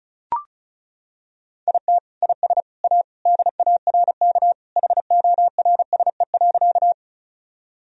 Morse code, heard as a keyed tone, about 36 wpm, 700 Hz. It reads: IT IS A DARK HORSE.
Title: Sentences from Top 1000 Words - Review Part 1 - 35wpm